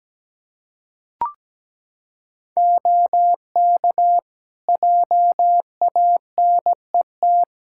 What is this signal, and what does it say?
Morse code, heard as a keyed tone, about 17 wpm, 700 Hz. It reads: OK JANET
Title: Practice QSO - Exchange 2 - Info - 17wpm